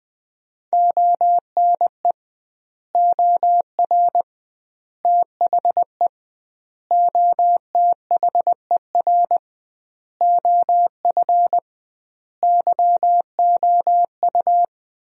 Morse code, heard as a keyed tone, about 20 wpm, 700 Hz. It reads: ONE OR THE OTHER OF YOU
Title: 2x Word-Spacing - Sentences from Top 400 Words - 20wpm